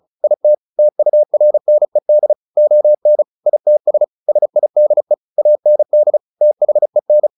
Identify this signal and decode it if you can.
Morse code, heard as a keyed tone, about 34 wpm, 600 Hz. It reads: IT TURNED ON ITS SIDE AND THEN
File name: the old man and the sea / the old man and the sea 1 35wpm